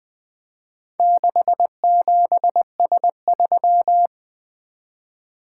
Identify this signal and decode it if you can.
Morse code, heard as a keyed tone, about 20 wpm, 700 Hz. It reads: 67S3
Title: Four Letter-Number Combinations - 20wpm